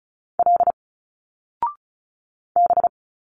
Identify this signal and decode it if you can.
Morse code, heard as a keyed tone, about 36 wpm, 700 Hz. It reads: L B
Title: Single Letters 35wpm